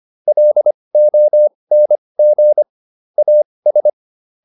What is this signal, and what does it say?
Morse code, heard as a keyed tone, about 25 wpm, spaced as for 20 wpm, 600 Hz. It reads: LONG AS